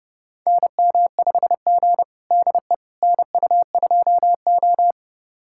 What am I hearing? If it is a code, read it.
Morse code, heard as a keyed tone, about 32 wpm, 700 Hz. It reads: NM5G DE NU2O
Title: USA Callsign DE Callsign - 4 Characters - 30wpm